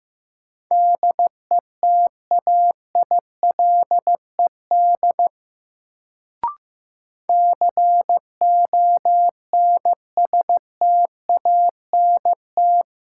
Morse code, heard as a keyed tone, about 15 wpm, 700 Hz. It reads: DETAILED CONSTANT
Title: Sentences from Top 1500 Words - New Word Drill - 15wpm